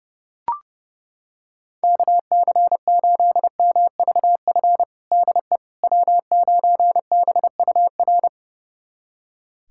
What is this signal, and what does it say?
Morse code, heard as a keyed tone, about 30 wpm, 700 Hz. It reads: KC8MVF DE W9BUR
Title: USA Callsign DE Callsign 30wpm